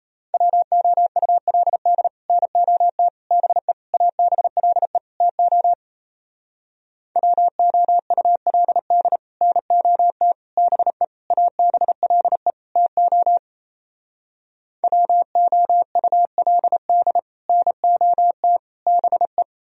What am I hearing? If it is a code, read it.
Morse code, heard as a keyed tone, about 38 wpm, 700 Hz. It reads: WOULD NOT BE ABLE TO WOULD NOT BE ABLE TO WOULD NOT BE